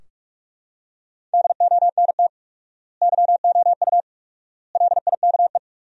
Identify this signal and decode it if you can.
Morse code, heard as a keyed tone, about 42 wpm, 700 Hz. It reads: DONT YOU LIKE
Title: 4x Word-Spacing - Sentences from Top 300 Words - 45wpm